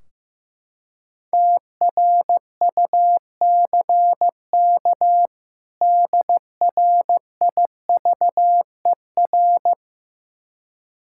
Morse code, heard as a keyed tone, about 15 wpm, 700 Hz. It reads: TRUCK DRIVER
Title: Practice QSO - Exchange 3 - More Info - 15wpm